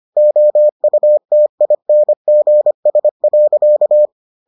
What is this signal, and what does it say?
Morse code, heard as a keyed tone, about 25 wpm, 600 Hz. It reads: OUTINGS.